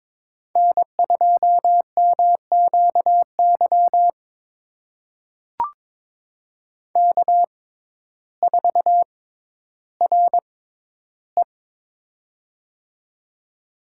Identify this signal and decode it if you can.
Morse code, heard as a keyed tone, about 22 wpm, spaced as for 12 wpm, 700 Hz. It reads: N2MQY K4RE
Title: USA Call Signs - 6x Character Spacing - 22wpm